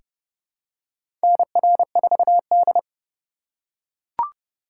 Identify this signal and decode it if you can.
Morse code, heard as a keyed tone, about 30 wpm, 700 Hz. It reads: NR4D
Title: USA Call Signs - 2x Character Spacing - 30wpm